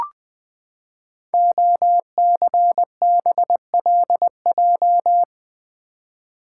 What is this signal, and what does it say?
Morse code, heard as a keyed tone, about 20 wpm, 700 Hz. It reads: OCBLJ
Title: Code Groups - 5 Letters - 20wpm